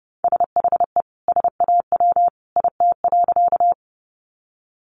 Morse code, heard as a keyed tone, about 30 wpm, 700 Hz. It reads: SHE SAW IT.
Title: Sentences from Top 300 Words 30wpm